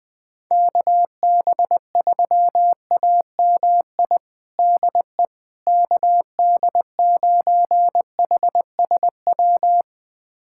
Morse code, heard as a keyed tone, about 20 wpm, 700 Hz. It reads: KB3AMI DE KD9HSW
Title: USA Callsign DE Callsign - 6 Characters - 20wpm